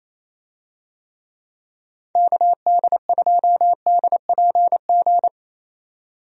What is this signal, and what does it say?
Morse code, heard as a keyed tone, about 28 wpm, 700 Hz. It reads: KD2DPG
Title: USA Call Signs - 6 Characters - 28wpm